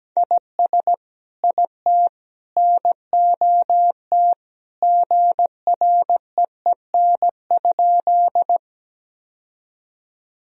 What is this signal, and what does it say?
Morse code, heard as a keyed tone, about 17 wpm, 700 Hz. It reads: IS IT NOT GREEN?
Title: Sentences from Top 1000 Words - Review Part 2 - 17wpm